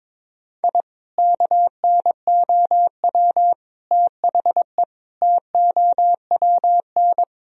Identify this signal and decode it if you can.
Morse code, heard as a keyed tone, about 22 wpm, 700 Hz. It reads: I KNOW THE TOWN
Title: Sentences from Top 1000 Words - Review Part 2 - 22wpm